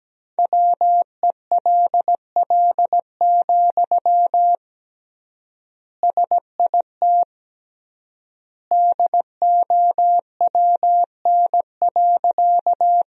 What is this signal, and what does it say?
Morse code, heard as a keyed tone, about 17 wpm, 700 Hz. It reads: WELL, SIT DOWN.
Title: 3x Word-Spacing - Sentences from Top 500 Words - 17wpm